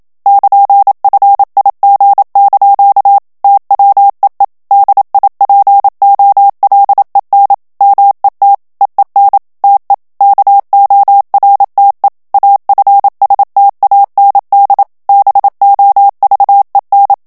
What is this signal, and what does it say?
Morse code, heard as a keyed tone, about 27 wpm, 800 Hz. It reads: <KN>FIG) TWEE DIPOLEN MET EEN TE KORTE AFSTAND BOVEN
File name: HOLLANDAIS texte clair 1 vitesse 26 mots minute